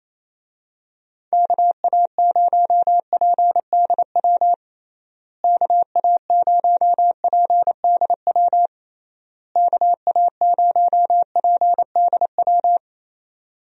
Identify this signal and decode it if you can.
Morse code, heard as a keyed tone, about 28 wpm, 700 Hz. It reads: KA0PDW KA0PDW KA0PDW
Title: USA Call Signs - 6 Characters - Repeated 3x - 28wpm